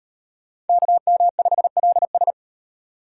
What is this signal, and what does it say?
Morse code, heard as a keyed tone, about 38 wpm, 700 Hz. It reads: KM5LS